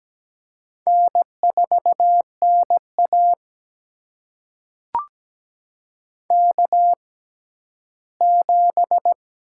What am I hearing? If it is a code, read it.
Morse code, heard as a keyed tone, about 17 wpm, spaced as for 12 wpm, 700 Hz. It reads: N4NA K7
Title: USA Call Signs - 4 Characters - 6x Character Spacing - 17wpm